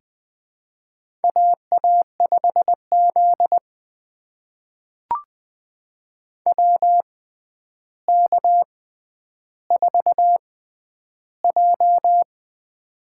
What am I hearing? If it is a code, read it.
Morse code, heard as a keyed tone, about 20 wpm, spaced as for 10 wpm, 700 Hz. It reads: AA5Z WK4J
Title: USA Call Signs - 4 Characters - 6x Character Spacing - 20wpm